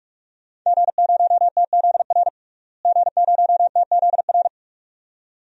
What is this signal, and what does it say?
Morse code, heard as a keyed tone, about 45 wpm, 700 Hz. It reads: G0TZR G0TZR
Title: DXCC-100 Call Signs - 5 Characters - Repeated 2x - 45wpm